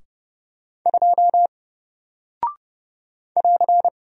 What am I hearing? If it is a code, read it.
Morse code, heard as a keyed tone, about 30 wpm, 700 Hz. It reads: 2 <AR>
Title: Single Character (Letter:Number:Punctuation:Prosign) Alt 30wpm